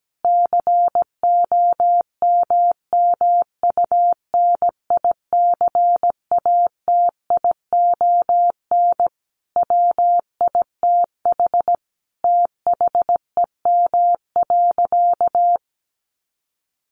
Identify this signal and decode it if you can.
Morse code, heard as a keyed tone, about 17 wpm, 700 Hz. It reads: COMMUNICATION WITH THEM.
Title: Sentences from Top 1400 Words 17wpm